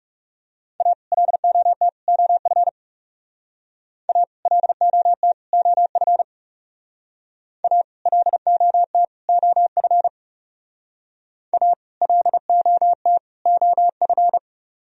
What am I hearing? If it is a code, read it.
Morse code, heard as a keyed tone, about 35 wpm, 700 Hz. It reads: A LOT OF A LOT OF A LOT OF A LOT OF
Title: Speed-Racing - Sets of 3 Words 30wpm